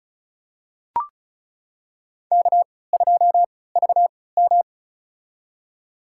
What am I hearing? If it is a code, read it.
Morse code, heard as a keyed tone, about 34 wpm, spaced as for 20 wpm, 690 Hz. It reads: K2VM
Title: USA Call Signs - 3x Character Spacing - 35wpm